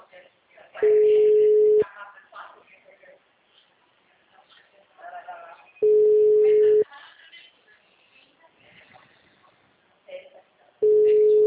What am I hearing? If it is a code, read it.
Ringback tone (Europe, 425 Hz), with speech.